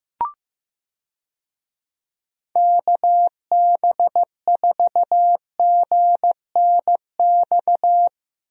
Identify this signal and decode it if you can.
Morse code, heard as a keyed tone, about 15 wpm, 700 Hz. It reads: KB4GNX
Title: USA Call Signs - 6 Characters - Repeated 2x - 15wpm